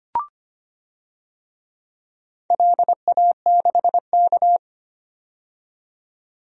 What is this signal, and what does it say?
Morse code, heard as a keyed tone, about 25 wpm, 700 Hz. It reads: LA6K
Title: DXCC-100 Call Signs - 4 Characters - 25wpm